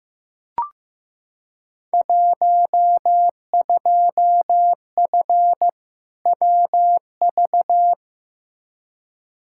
Morse code, heard as a keyed tone, about 15 wpm, 700 Hz. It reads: 12F WV
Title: Field Day Exchanges Encore - 15wpm